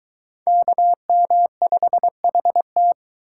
Morse code, heard as a keyed tone, about 20 wpm, 700 Hz. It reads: KM5HT